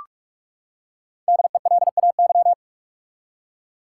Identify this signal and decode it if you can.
Morse code, heard as a keyed tone, about 46 wpm, 700 Hz. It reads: DELAY
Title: Five Letter Words - 45wpm